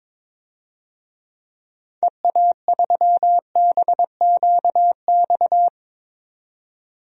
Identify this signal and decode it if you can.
Morse code, heard as a keyed tone, about 22 wpm, 700 Hz. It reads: EA3BQX
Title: DXCC-100 Call Signs - 6 Characters - 22wpm